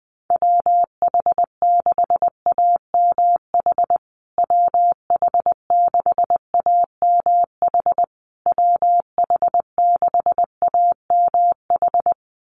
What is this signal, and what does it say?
Morse code, heard as a keyed tone, about 20 wpm, 700 Hz. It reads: WH6AMH WH6AMH WH6AMH
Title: USA Call Signs Repeated 3x 20wpm